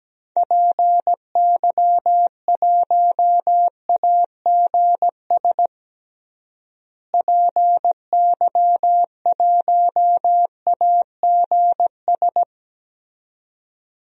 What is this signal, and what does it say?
Morse code, heard as a keyed tone, about 17 wpm, 700 Hz. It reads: PY1AGS PY1AGS
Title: DXCC-100 Call Signs - 6 Characters - Repeated 2x - 17wpm